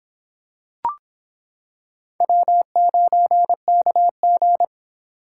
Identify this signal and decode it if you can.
Morse code, heard as a keyed tone, about 25 wpm, 700 Hz. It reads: W9KG